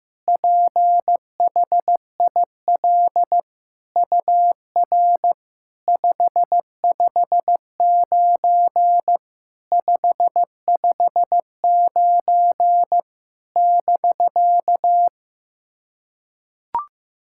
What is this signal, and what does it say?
Morse code, heard as a keyed tone, about 15 wpm, 700 Hz. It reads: PHIL UR 559 559 <BK>
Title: SOTA QSO Practice 15wpm